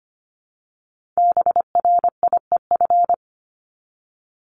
Morse code, heard as a keyed tone, about 25 wpm, 700 Hz. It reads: BRIEF
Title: LY Suffix - Top 100 Words with Base Words 25wpm